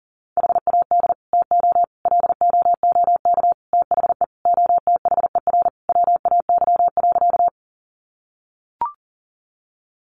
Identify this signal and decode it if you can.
Morse code, heard as a keyed tone, about 37 wpm, 700 Hz. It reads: HAD TO LOOK THE OTHER WAY.
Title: Sentences from Top 100 Words 40wpm